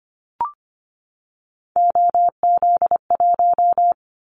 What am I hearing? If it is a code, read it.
Morse code, heard as a keyed tone, about 25 wpm, 700 Hz. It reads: OZ1
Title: Callsign Prefixes for Top 100 DXCC - 25wpm